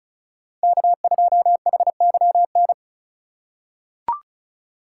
Morse code, heard as a keyed tone, about 34 wpm, 700 Hz. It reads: K2HYN
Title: USA Call Signs - 5 Characters - 3x Character Spacing - 35wpm